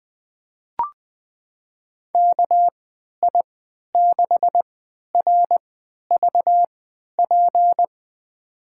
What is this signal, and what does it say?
Morse code, heard as a keyed tone, about 20 wpm, spaced as for 11 wpm, 700 Hz. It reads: KI6RVP
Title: USA Call Signs - 6 Characters - 3x Character Spacing - 20wpm